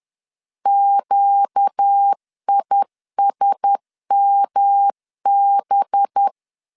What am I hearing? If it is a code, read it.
Morse code, heard as a keyed tone, about 10 wpm, 800 Hz. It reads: QISMB
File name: LEZIONI / 9 2867 10wpm 2733